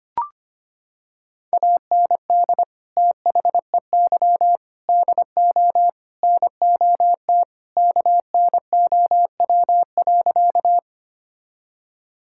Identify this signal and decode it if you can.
Morse code, heard as a keyed tone, about 25 wpm, 700 Hz. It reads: AND THEY DO NOT KNOW.
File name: practice / 25wpm / Sentences with 4-Word Phrases - 25wpm